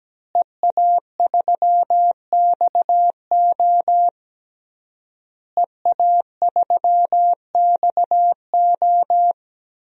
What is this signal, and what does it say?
Morse code, heard as a keyed tone, about 17 wpm, 700 Hz. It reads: EA3XO EA3XO
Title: DXCC-100 Call Signs - 5 Characters - Repeated 3x - 17wpm